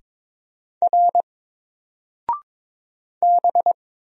Morse code, heard as a keyed tone, about 22 wpm, 700 Hz. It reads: R B
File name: Single-Letter-Number-22wpm